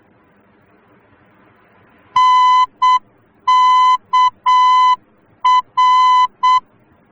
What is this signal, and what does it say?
Morse code, heard as a keyed tone, about 7 wpm, 1010 Hz. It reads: NKR